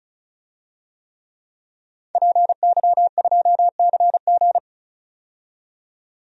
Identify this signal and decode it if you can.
Morse code, heard as a keyed tone, about 34 wpm, 700 Hz. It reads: PY2CG
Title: DXCC-100 Call Signs - 5 Characters - 35wpm